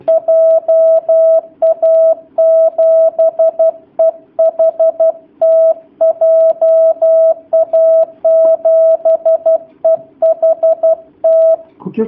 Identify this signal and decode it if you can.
Morse code, heard as a keyed tone, about 12 wpm, 650 Hz. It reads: JA7EHT JA7EHT
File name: jh8cbh / 2019cw / sp11- w